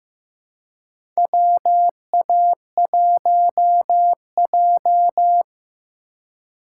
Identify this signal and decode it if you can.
Morse code, heard as a keyed tone, about 15 wpm, 700 Hz. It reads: WA1J
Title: USA Call Signs - 4 Characters - 2x Character Spacing - 15wpm